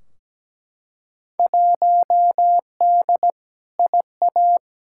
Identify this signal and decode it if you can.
Morse code, heard as a keyed tone, about 17 wpm, 700 Hz. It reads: 1D IA
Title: Field Day Exchanges - 17wpm